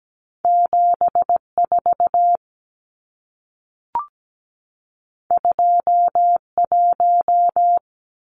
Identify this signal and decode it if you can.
Morse code, heard as a keyed tone, about 17 wpm, 700 Hz. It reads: 74 21
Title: Numbers - 2-Digit 17wpm